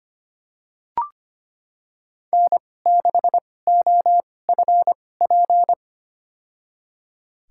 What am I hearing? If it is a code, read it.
Morse code, heard as a keyed tone, about 25 wpm, spaced as for 18 wpm, 700 Hz. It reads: N6OFP
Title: USA Call Signs - 5 Characters - 2x Character Spacing - 25wpm